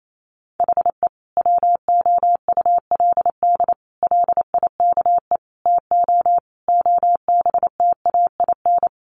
Morse code, heard as a keyed tone, about 28 wpm, 700 Hz. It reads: HE WOULD LIKE TO OBTAIN
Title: Sentences from Top 1300 Words 28wpm